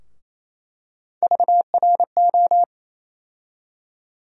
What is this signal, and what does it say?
Morse code, heard as a keyed tone, about 28 wpm, 700 Hz. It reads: VRO
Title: USA Call Sign Postfixes - 3 Letters 28wpm